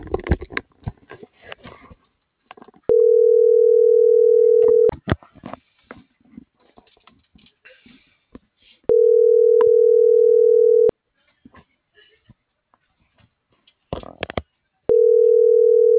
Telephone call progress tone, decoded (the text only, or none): ringback tone